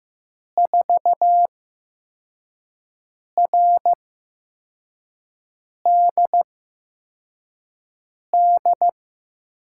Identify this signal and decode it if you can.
Morse code, heard as a keyed tone, about 15 wpm, spaced as for 4 wpm, 700 Hz. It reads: 4RDD
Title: USA Call Signs - 6 Characters - 8x Character Spacing - 15wpm